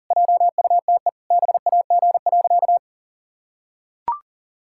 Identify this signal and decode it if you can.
Morse code, heard as a keyed tone, about 37 wpm, 700 Hz. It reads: JUTE BAG.
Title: Pangram Challenge v2 - 40wpm